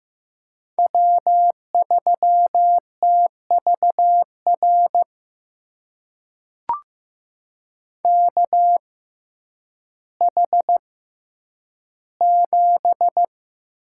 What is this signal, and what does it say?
Morse code, heard as a keyed tone, about 15 wpm, spaced as for 9 wpm, 700 Hz. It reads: W3TVR KH7
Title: USA Call Signs - 5 Characters - 6x Character Spacing - 15wpm